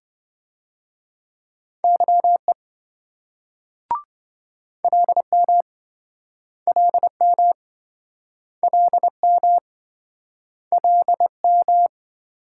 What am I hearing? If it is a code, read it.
Morse code, heard as a keyed tone, about 25 wpm, 700 Hz. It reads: YE LM LM LM LM
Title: Speed-Racing Multi-letter Phonemes - 20wpm